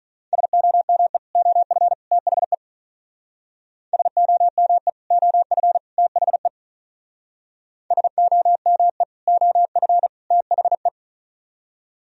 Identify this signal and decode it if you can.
Morse code, heard as a keyed tone, about 41 wpm, 700 Hz. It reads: SOME OF THE SOME OF THE SOME OF THE
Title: Speed-Racing - Sets of 3 Words 35wpm